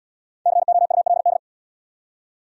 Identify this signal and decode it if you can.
Morse code, heard as a keyed tone, about 67 wpm, 700 Hz. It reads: COULD